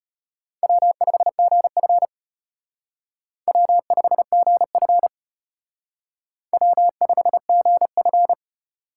Morse code, heard as a keyed tone, about 33 wpm, 700 Hz. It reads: W5GF W5GF W5GF